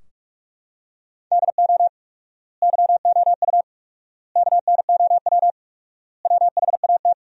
Morse code, heard as a keyed tone, about 46 wpm, 700 Hz. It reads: DO YOU KNOW WHAT
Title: 4x Word-Spacing - Sentences from Top 100 Words - 45wpm